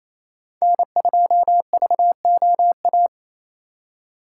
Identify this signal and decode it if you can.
Morse code, heard as a keyed tone, about 35 wpm, 700 Hz. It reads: N2VOA
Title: Speed Racing - USA 1x3 Call Signs 28wpm